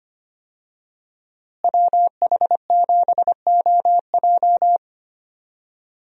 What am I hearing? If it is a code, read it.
Morse code, heard as a keyed tone, about 25 wpm, 700 Hz. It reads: WH7OJ
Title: USA Call Signs - 5 Characters - Repeated 3x - 25wpm